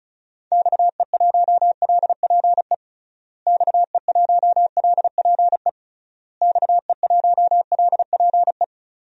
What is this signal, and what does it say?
Morse code, heard as a keyed tone, about 36 wpm, 700 Hz. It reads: XE1LPE XE1LPE XE1LPE
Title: DXCC-100 Call Signs - 6 Characters - Repeated 3x - 35wpm